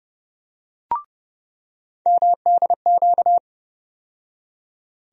Morse code, heard as a keyed tone, about 30 wpm, 700 Hz. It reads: MDQ
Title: USA Call Sign Postfixes - 3 Letters 30wpm